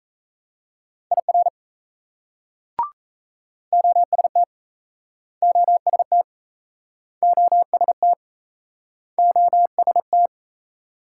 Morse code, heard as a keyed tone, about 35 wpm, 700 Hz. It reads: IR OST OST OST OST
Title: Speed-Racing Multi-letter Phonemes - 28wpm